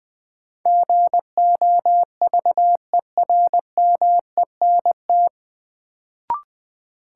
Morse code, heard as a keyed tone, about 20 wpm, 700 Hz. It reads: GOVERMENT
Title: Sentences from Top 1300 Words - New Word Drill - 20wpm